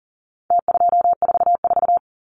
Morse code, heard as a keyed tone, about 40 wpm, 700 Hz. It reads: T244
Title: Numbers - Contest Style - 4-Digit 40wpm